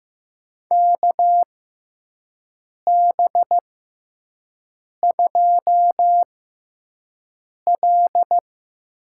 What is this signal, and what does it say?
Morse code, heard as a keyed tone, about 15 wpm, spaced as for 5 wpm, 700 Hz. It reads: KB2L
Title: USA Call Signs - 6 Characters - 6x Character Spacing - 15wpm